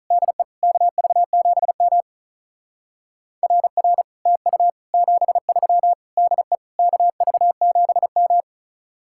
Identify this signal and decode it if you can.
Morse code, heard as a keyed tone, about 44 wpm, 700 Hz. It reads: DE KV7M RR TU 73 DE KV7M
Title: Speed-Racing Common QSO Phrases Encore - 35wpm